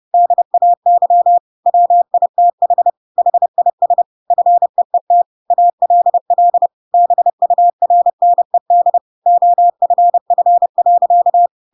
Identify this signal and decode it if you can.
Morse code, heard as a keyed tone, about 30 wpm, 700 Hz. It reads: DAY WITH HIS FEET ALL BURNED OFF.
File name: Pinocchio / Pinocchio ch6 30 30 ik1hge